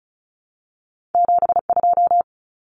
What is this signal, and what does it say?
Morse code, heard as a keyed tone, about 36 wpm, 700 Hz. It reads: 72
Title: Numbers - 2-Digit 35wpm